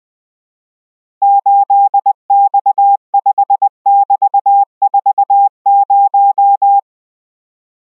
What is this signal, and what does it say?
Morse code, heard as a keyed tone, about 20 wpm, 800 Hz. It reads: 8X5<BT>40